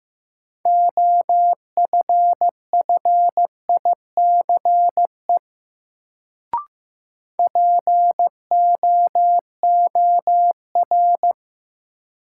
Morse code, heard as a keyed tone, about 16 wpm, 700 Hz. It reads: OFFICE POOR
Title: Sentences from Top 1000 Words - New Word Drill - 15wpm